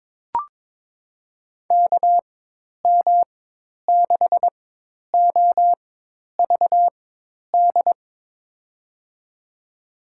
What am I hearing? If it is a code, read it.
Morse code, heard as a keyed tone, about 22 wpm, spaced as for 10 wpm, 700 Hz. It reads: KM6OVD
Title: USA Call Signs - 6 Characters - 4x Character Spacing - 22wpm